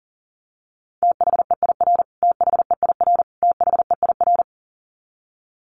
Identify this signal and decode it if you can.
Morse code, heard as a keyed tone, about 40 wpm, 700 Hz. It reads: THEIR THEIR THEIR
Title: Top 500 Words Repeated 40wpm